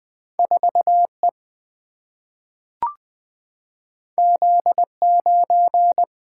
Morse code, heard as a keyed tone, about 20 wpm, 700 Hz. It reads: 4E Z9
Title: Single-Letter Single-Number 20wpm